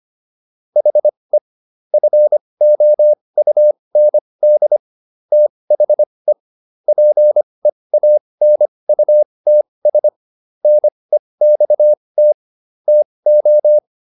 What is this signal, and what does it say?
Morse code, heard as a keyed tone, about 25 wpm, spaced as for 20 wpm, 600 Hz. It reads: HE FOUND THE PEANUTS NEXT TO